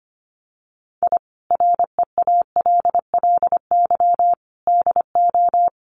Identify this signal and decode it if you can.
Morse code, heard as a keyed tone, about 25 wpm, 700 Hz. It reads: I REALLY DO
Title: Sets of 5 Words Encore2 25wpm